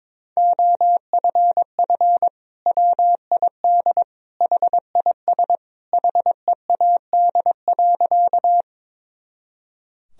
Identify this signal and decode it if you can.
Morse code, heard as a keyed tone, about 22 wpm, 700 Hz. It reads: OFF WID HIS HEAD.
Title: Sentences from Top 300 Words and Abbreviations 22wpm